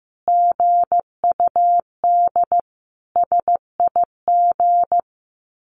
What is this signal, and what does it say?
Morse code, heard as a keyed tone, about 15 wpm, 700 Hz. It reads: GUD SIG
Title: Common QSO Phrases 15wpm